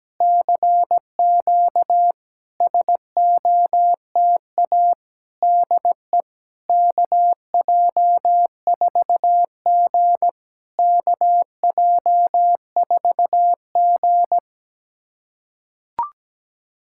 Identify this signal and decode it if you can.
Morse code, heard as a keyed tone, about 17 wpm, 660 Hz. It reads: CQ SOTA DE KJ4G KJ4G
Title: SOTA QSO Practice 17wpm